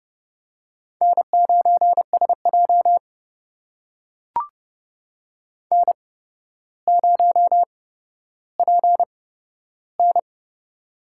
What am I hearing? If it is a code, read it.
Morse code, heard as a keyed tone, about 30 wpm, spaced as for 13 wpm, 700 Hz. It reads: N9SJ N0PN
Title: USA Call Signs - 4 Characters - 8x Character Spacing - 30wpm